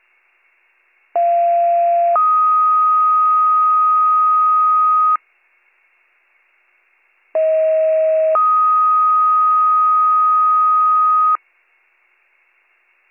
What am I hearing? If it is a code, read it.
Two-tone page, heard as a one-second tone, then a three-second tone, about 690 then 1250 Hz, about 640 then 1250 Hz. It is heard from 1.1 to 11.4 s.